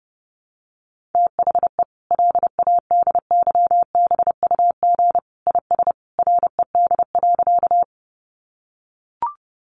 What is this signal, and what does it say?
Morse code, heard as a keyed tone, about 30 wpm, 700 Hz. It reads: THE LADYBUG IS RED.
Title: Additional Easy Phrases 30wpm